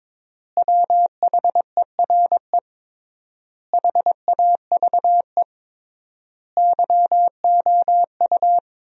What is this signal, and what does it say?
Morse code, heard as a keyed tone, about 22 wpm, 700 Hz. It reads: WHERE HAVE YOU
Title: 3x Word-Spacing - Sentences from Top 400 Words - 22wpm